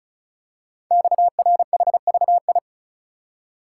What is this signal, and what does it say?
Morse code, heard as a keyed tone, about 36 wpm, 690 Hz. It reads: XRHVI
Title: Code Groups - 5 Letters - 35wpm